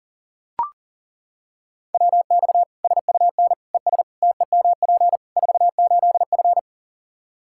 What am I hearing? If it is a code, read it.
Morse code, heard as a keyed tone, about 40 wpm, 700 Hz. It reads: WX SUN ES TEMP 48F
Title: K1USN SST QSO Practice 40wpm